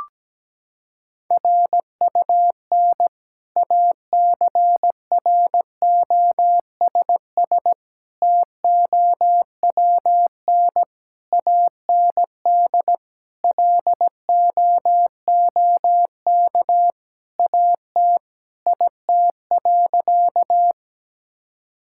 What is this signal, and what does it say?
Morse code, heard as a keyed tone, about 17 wpm, 700 Hz. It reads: RUN ACROSS TOWN AND LOOK AT IT.
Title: Sentences from Top 1000 Words - Review Part 5 - 17wpm